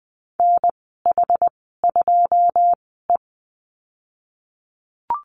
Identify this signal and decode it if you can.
Morse code, heard as a keyed tone, about 20 wpm, spaced as for 15 wpm, 700 Hz. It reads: NH2E
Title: 082 Lesson 14g - Number 2 - Call Signs 20wpm